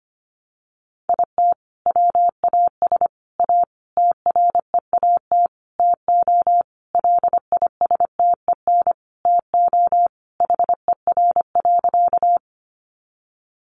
Morse code, heard as a keyed tone, about 25 wpm, 700 Hz. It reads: IT WAS A TREAT TO LISTEN TO HER.
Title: Sentences from Top 1300 Words 25wpm